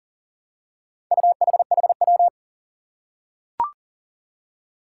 Morse code, heard as a keyed tone, about 40 wpm, 700 Hz. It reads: UHHW
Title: Four Challenging Letter-Number Combinations 40wpm